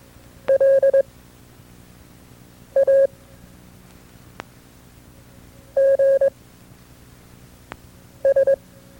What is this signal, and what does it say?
Morse code, heard as a keyed tone, about 22 wpm, spaced as for 4 wpm, 540 Hz. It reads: LAGS
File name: FreeFiles / CW-practice / KB6MT-tape2-side2